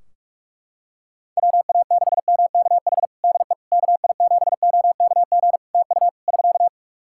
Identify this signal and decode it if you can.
Morse code, heard as a keyed tone, about 47 wpm, 700 Hz. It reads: WA6MKH DE KI7OKG TU <SK>
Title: Practice QSO - Exchange 4 - Ending - 45wpm